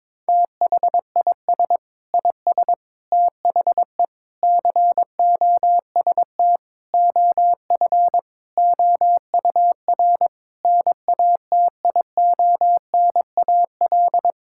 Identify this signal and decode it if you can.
Morse code, heard as a keyed tone, about 22 wpm, 700 Hz. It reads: THIS IS THE COST OF OUR NATIONAL
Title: Sentences from Top 1000 Words - Review Part 5 - 22wpm